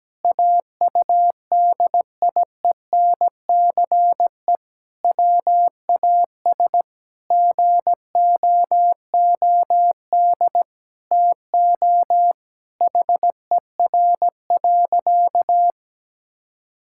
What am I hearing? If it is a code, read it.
Morse code, heard as a keyed tone, about 17 wpm, 700 Hz. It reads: AUDIENCE WAS GOOD TO HER.
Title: Sentences from Top 1000 Words - Review Part 5 - 17wpm